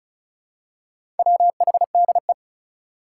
Morse code, heard as a keyed tone, about 34 wpm, 700 Hz. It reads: WHDE
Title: Four Challenging Letter-Number Combinations 35wpm